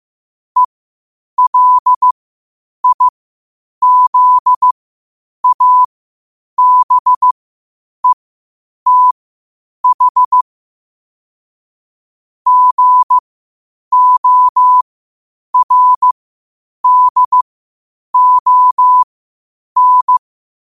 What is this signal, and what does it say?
Morse code, heard as a keyed tone, about 15 wpm, spaced as for 8 wpm, 1000 Hz. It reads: ELIZABETH GORDON
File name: qotdfiles / 8wpm